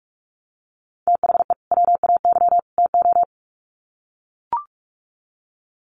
Morse code, heard as a keyed tone, about 44 wpm, 700 Hz. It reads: THE WAY TO
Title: Sets of 3 Words 45wpm